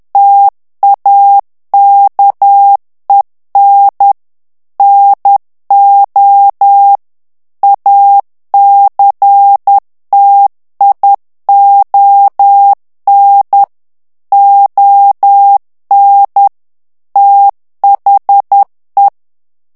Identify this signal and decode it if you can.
Morse code, heard as a keyed tone, about 10 wpm, 800 Hz. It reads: TAKEN NO ACTION ON THE
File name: ANGLAIS texte clair 3 vitesse 10 mots minute